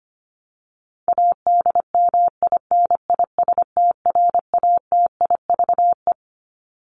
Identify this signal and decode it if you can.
Morse code, heard as a keyed tone, about 25 wpm, 700 Hz. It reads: ADMINISTRATIVE
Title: Top 1501-1750 Words 25wpm